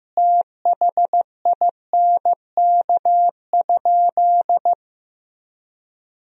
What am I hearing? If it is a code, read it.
Morse code, heard as a keyed tone, about 15 wpm, 700 Hz. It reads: THINK?
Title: Sentences from Top 200 Words and Abbreviations 15wpm